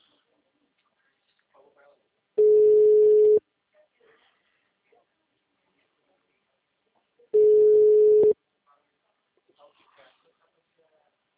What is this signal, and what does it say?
Ringback tone (Europe, 425 Hz), with speech.
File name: out- -2008